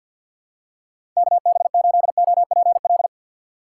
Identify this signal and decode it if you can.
Morse code, heard as a keyed tone, about 53 wpm, 700 Hz. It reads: KB8OPL
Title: USA Call Signs - 6 Characters - 8x Character Spacing - 50wpm